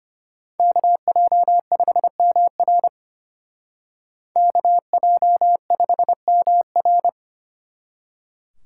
Morse code, heard as a keyed tone, about 32 wpm, 700 Hz. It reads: KJ5MR KJ5MR